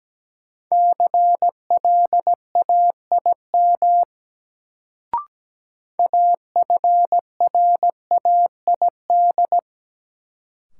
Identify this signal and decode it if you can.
Morse code, heard as a keyed tone, about 17 wpm, 700 Hz. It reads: CLAIM AFRAID
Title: Sentences from Top 1500 Words - New Word Drill Alt - 17wpm